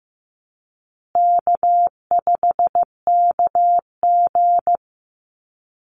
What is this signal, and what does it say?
Morse code, heard as a keyed tone, about 15 wpm, 700 Hz. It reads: K5KG
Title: Easy Call Signs Repeated 3x - 15wpm